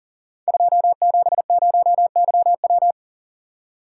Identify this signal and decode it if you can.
Morse code, heard as a keyed tone, about 40 wpm, 700 Hz. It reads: 270YW
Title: Code Groups - 5 Letter:Numbers - 40wpm